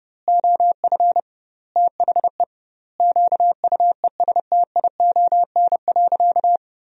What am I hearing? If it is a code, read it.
Morse code, heard as a keyed tone, about 30 wpm, 700 Hz. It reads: OF THE QUESTION.
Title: 2x Word-Spacing - Sentences from Top 400 Words - 30wpm